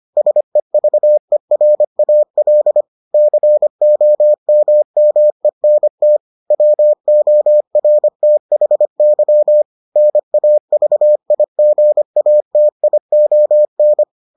Morse code, heard as a keyed tone, about 25 wpm, 600 Hz. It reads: SEVERAL COMMENT WORTHY NAVIGATION